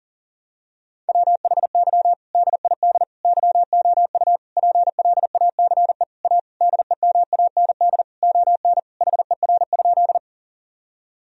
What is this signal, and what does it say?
Morse code, heard as a keyed tone, about 40 wpm, 700 Hz. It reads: WHY DID YOU PLACE A DEMAND ON HER?
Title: Sentences from Top 1000 Words - Review Part 4 - 40wpm